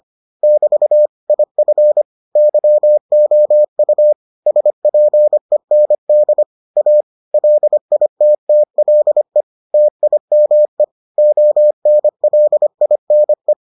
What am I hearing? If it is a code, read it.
Morse code, heard as a keyed tone, about 25 wpm, 600 Hz. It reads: <BT> IF YOU SPEND A LITTLE TIME ONLINE